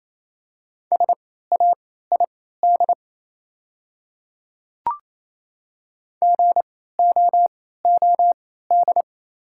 Morse code, heard as a keyed tone, about 28 wpm, spaced as for 16 wpm, 700 Hz. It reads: SAID GOOD
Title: 3x Character Spacing - Top 200 Words - 28wpm